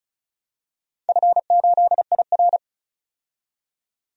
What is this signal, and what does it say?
Morse code, heard as a keyed tone, about 36 wpm, 700 Hz. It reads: F8IR
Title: DXCC-100 Call Signs - 4 Characters - 35wpm